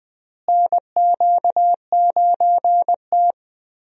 Morse code, heard as a keyed tone, about 20 wpm, 700 Hz. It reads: NQ9T
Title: USA Call Signs - 4 Characters - Repeated 2x - 20wpm